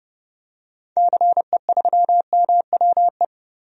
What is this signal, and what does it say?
Morse code, heard as a keyed tone, about 30 wpm, 700 Hz. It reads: CE3MWE
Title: DXCC-100 Call Signs - 6 Characters - 30wpm